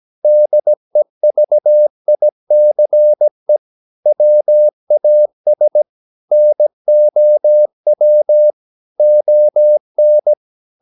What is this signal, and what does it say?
Morse code, heard as a keyed tone, about 17 wpm, 600 Hz. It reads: DEVICE WAS NOW ON
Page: Romanen Never af Ken Follet på engelsk.